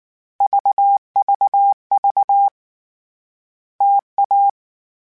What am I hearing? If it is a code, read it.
Morse code, heard as a keyed tone, about 19 wpm, 800 Hz. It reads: VVV TA
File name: Variable Speed / Variable Speed-mp3s / T AOEI's sp @18